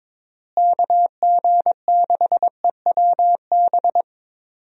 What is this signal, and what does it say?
Morse code, heard as a keyed tone, about 22 wpm, 700 Hz. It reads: KG6EWB
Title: USA Call Signs - 6 Characters - Repeated 3x - 22wpm